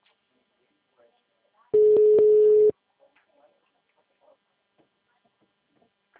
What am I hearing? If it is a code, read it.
Ringback tone (Europe, 425 Hz), with speech.